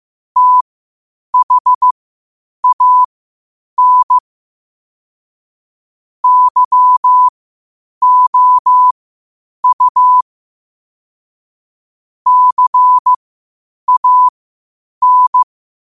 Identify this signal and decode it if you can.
Morse code, heard as a keyed tone, about 15 wpm, spaced as for 8 wpm, 1000 Hz. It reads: THAN YOU CAN
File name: qotdfiles / 8wpm